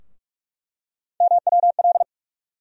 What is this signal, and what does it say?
Morse code, heard as a keyed tone, about 46 wpm, 700 Hz. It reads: MWL
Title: USA Call Sign Postfixes - 3 Letters 45wpm